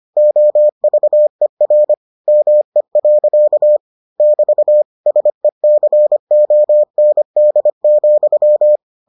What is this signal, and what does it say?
Morse code, heard as a keyed tone, about 25 wpm, 600 Hz. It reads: OVER ME. <BT> SECOND,